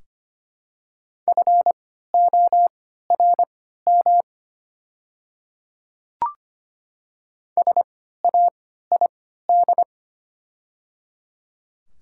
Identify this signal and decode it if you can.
Morse code, heard as a keyed tone, about 25 wpm, spaced as for 14 wpm, 690 Hz. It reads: FORM SAID
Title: 3x Character Spacing - Top 200 Words - 25wpm